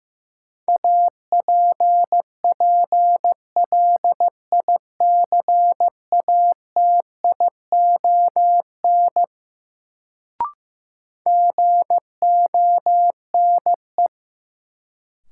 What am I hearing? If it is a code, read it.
Morse code, heard as a keyed tone, about 16 wpm, 700 Hz. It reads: APPLICATION GONE
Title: Sentences from Top 1200 Words - New Word Drill - 15wpm